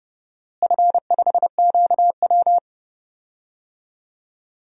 Morse code, heard as a keyed tone, about 29 wpm, 700 Hz. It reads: F5QW
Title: DXCC-100 Call Signs - 4 Characters - Repeated 3x - 30wpm